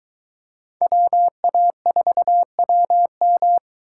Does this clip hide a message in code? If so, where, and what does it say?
Yes: Morse code from 0.8 to 3.6 s, heard as a keyed tone, about 22 wpm, 700 Hz. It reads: WA4WM